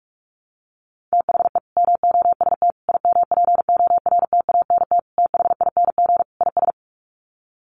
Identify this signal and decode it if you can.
Morse code, heard as a keyed tone, about 44 wpm, 700 Hz. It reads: THE MOST IMPORTANT THING IS
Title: Sets of 5 Words Encore 45wpm